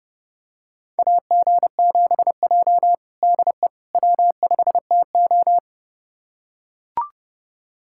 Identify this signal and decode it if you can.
Morse code, heard as a keyed tone, about 32 wpm, 700 Hz. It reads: AG7J DE W5TO
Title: USA Callsign DE Callsign - 4 Characters - 30wpm